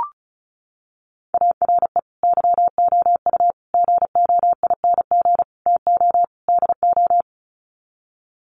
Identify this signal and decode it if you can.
Morse code, heard as a keyed tone, about 34 wpm, 700 Hz. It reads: ARE YOU GOING TO DO
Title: Sets of 5 Words 35wpm